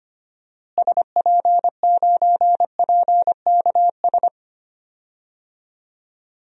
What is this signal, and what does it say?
Morse code, heard as a keyed tone, about 25 wpm, 700 Hz. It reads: SP9PKS
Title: DXCC-100 Call Signs - 6 Characters - Repeated 2x - 25wpm